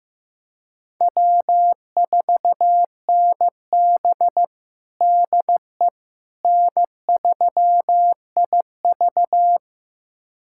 Morse code, heard as a keyed tone, about 16 wpm, 700 Hz. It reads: W4NB DE N3IV
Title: USA Callsign DE Callsign - 4 Characters - 15wpm